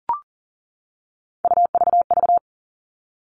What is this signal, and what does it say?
Morse code, heard as a keyed tone, about 40 wpm, 700 Hz. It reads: UVV
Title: All Cut Numbers - 3-Digit 40wpm